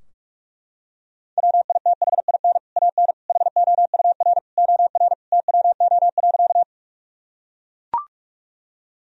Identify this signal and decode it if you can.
Morse code, heard as a keyed tone, about 41 wpm, 700 Hz. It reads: WITHIN AN HOUR OR TWO.
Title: Sentences from Top 1000 Words - Review Part 4 - 45wpm